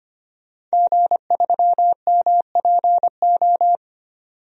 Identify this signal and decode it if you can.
Morse code, heard as a keyed tone, about 25 wpm, 700 Hz. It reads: G3MPO
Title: DXCC-100 Call Signs - 5 Characters - 25wpm